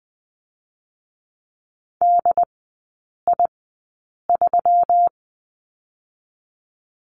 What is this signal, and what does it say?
Morse code, heard as a keyed tone, about 20 wpm, spaced as for 8 wpm, 700 Hz. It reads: DI3
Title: 121 Lesson 20e - Number 3 - Three Characters 20wpm